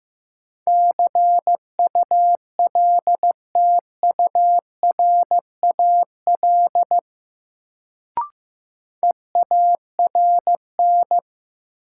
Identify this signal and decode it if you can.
Morse code, heard as a keyed tone, about 15 wpm, 700 Hz. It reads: CULTURAL EARN
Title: Sentences from Top 1500 Words - New Word Drill - 15wpm